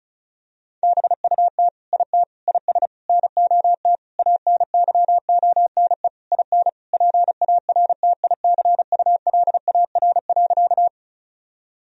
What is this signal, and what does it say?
Morse code, heard as a keyed tone, about 36 wpm, 700 Hz. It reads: BUT IT IS NOT ANYONE IN PARTICULAR.
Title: Sentences from Top 1000 Words - Review Part 4 - 35wpm